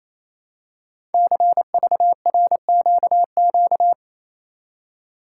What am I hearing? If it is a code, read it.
Morse code, heard as a keyed tone, about 27 wpm, 700 Hz. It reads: CVRQQ
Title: Code Groups - 5 Letters - 28wpm